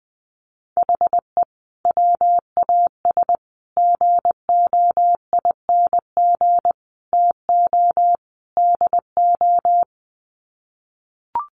Morse code, heard as a keyed tone, about 20 wpm, 700 Hz. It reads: HE WAS GOING TO DO
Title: Sets of 5 Words Encore2 20wpm